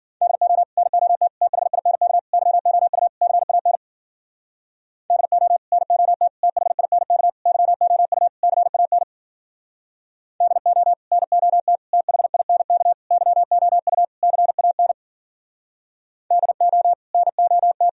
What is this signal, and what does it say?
Morse code, heard as a keyed tone, about 52 wpm, 700 Hz. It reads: DO NOT THINK YOU CAN DO NOT THINK YOU CAN DO NOT THINK YOU CAN DO NOT